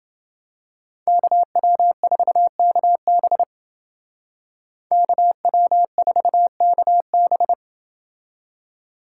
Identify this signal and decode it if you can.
Morse code, heard as a keyed tone, about 28 wpm, 700 Hz. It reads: KW4KB KW4KB